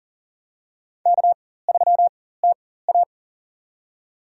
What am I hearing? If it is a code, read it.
Morse code, heard as a keyed tone, about 40 wpm, spaced as for 19 wpm, 700 Hz. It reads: K3TA
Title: USA Call Signs - 4 Characters - 4x Character Spacing - 40wpm